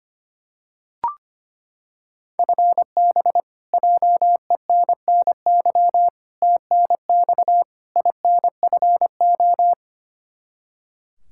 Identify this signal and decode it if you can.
Morse code, heard as a keyed tone, about 25 wpm, 700 Hz. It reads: FB JENNY TNX INFO
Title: Practice QSO - Exchange 3 - More Info - 25wpm